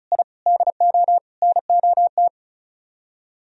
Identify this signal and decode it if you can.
Morse code, heard as a keyed tone, about 44 wpm, 700 Hz. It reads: I DO NOT
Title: Speed-Racing - Sets of 4 Words 35wpm